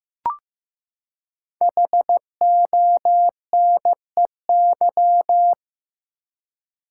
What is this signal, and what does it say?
Morse code, heard as a keyed tone, about 15 wpm, 700 Hz. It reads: HONEY
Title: Five Letter Words - 15wpm